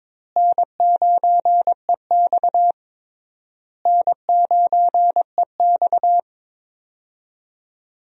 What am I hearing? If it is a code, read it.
Morse code, heard as a keyed tone, about 22 wpm, 700 Hz. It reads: N9EX N9EX
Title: USA Call Signs - 4 Characters - Repeated 3x - 22wpm